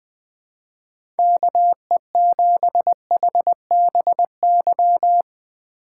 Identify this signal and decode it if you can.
Morse code, heard as a keyed tone, about 20 wpm, 700 Hz. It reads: KE7HBY
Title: USA Call Signs - 6 Characters - 2x Character Spacing - 20wpm